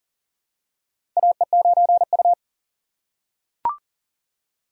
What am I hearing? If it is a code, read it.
Morse code, heard as a keyed tone, about 40 wpm, 700 Hz. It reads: AE9U
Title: USA Call Signs - 4 Characters - 3x Character Spacing - 40wpm